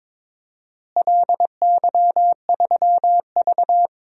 Morse code, heard as a keyed tone, about 21 wpm, 700 Hz. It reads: LY3V
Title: DXCC-100 Call Signs - 4 Characters - Repeated 2x - 22wpm